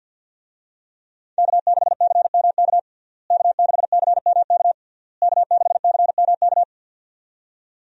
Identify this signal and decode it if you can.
Morse code, heard as a keyed tone, about 53 wpm, 700 Hz. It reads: K6CMK K6CMK K6CMK
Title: USA Call Signs - 5 Characters - Repeated 3x - 50wpm